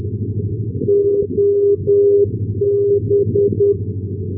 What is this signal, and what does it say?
Morse code, heard as a keyed tone, about 10 wpm, 400 Hz. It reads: OB
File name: europe / f-2049 ndb marseille-provence ob-395-1305-d-menorca-rb